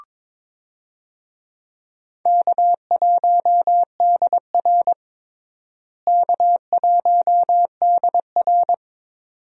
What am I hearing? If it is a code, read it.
Morse code, heard as a keyed tone, about 22 wpm, 700 Hz. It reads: K1DR K1DR
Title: USA Call Signs - 4 Characters - Repeated 2x - 22wpm